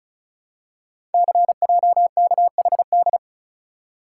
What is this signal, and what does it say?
Morse code, heard as a keyed tone, about 36 wpm, 700 Hz. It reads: CJKHD
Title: Code Groups - 5 Letter:Numbers - 35wpm